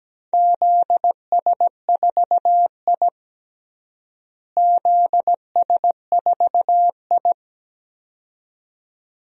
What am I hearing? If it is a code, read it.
Morse code, heard as a keyed tone, about 17 wpm, 700 Hz. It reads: ZS4I ZS4I
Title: DXCC-100 Call Signs - 4 Characters - Repeated 2x - 17wpm